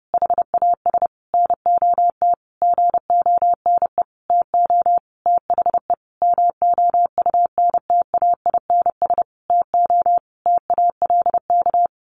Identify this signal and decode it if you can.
Morse code, heard as a keyed tone, about 28 wpm, 700 Hz. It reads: HAS NOT GONE TO THE MOUNTAINS TO TALK
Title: A Sound Of Thunder 30wpm